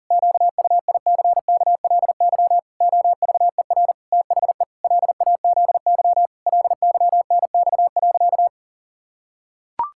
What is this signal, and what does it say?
Morse code, heard as a keyed tone, about 40 wpm, 700 Hz. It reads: QUICKLY OVER THE LAZY LYNX.
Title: Pangram Challenge v2 - 40wpm